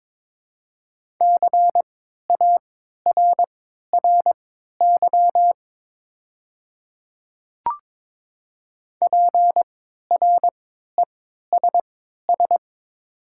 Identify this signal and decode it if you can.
Morse code, heard as a keyed tone, about 22 wpm, spaced as for 13 wpm, 700 Hz. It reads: CARRY PRESS
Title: 3x Character Spacing - Top 300 Words - 22wpm